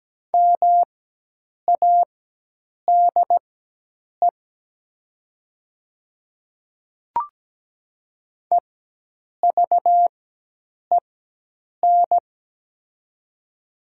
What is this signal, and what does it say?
Morse code, heard as a keyed tone, about 17 wpm, spaced as for 8 wpm, 700 Hz. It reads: MADE EVEN
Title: 4x Character Spacing - Top 300 Words - 17wpm